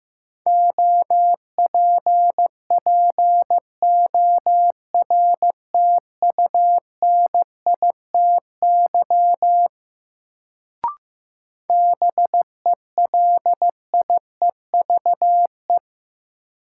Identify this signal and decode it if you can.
Morse code, heard as a keyed tone, about 15 wpm, 700 Hz. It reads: OPPORTUNITY BELIEVE
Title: Sentences from Top 1000 Words - New Word Drill - 15wpm